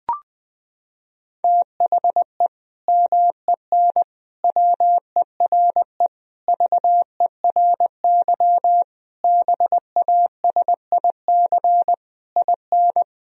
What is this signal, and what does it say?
Morse code, heard as a keyed tone, about 20 wpm, 700 Hz. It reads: THE MEN WERE VERY BASIC IN
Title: Sentences from Top 1000 Words - Review Part 3 - 20wpm